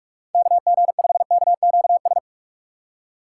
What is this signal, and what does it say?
Morse code, heard as a keyed tone, about 47 wpm, 700 Hz. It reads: KG5KQS
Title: USA Call Signs - 6 Characters - 6x Character Spacing - 45wpm